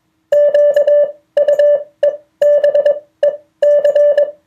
Morse code, heard as a keyed tone, about 21 wpm, spaced as for 16 wpm, 580 Hz. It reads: QUEBEC